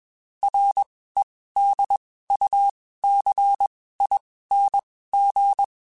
Morse code, heard as a keyed tone, about 21 wpm, spaced as for 15 wpm, 800 Hz. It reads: REDUCING